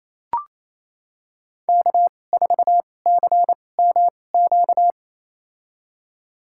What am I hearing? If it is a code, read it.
Morse code, heard as a keyed tone, about 28 wpm, spaced as for 20 wpm, 700 Hz. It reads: K4CMQ
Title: USA Call Signs - 5 Characters - 2x Character Spacing - 28wpm